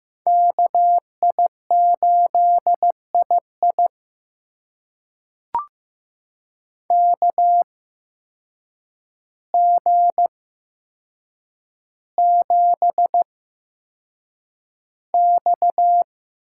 Morse code, heard as a keyed tone, about 15 wpm, spaced as for 7 wpm, 700 Hz. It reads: KI8II KG7X
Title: USA Call Signs - 5 Characters - 8x Character Spacing - 15wpm